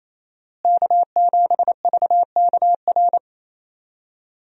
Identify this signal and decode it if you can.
Morse code, heard as a keyed tone, about 28 wpm, 700 Hz. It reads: K7VKR
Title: USA Call Signs - 5 Characters - Repeated 2x - 28wpm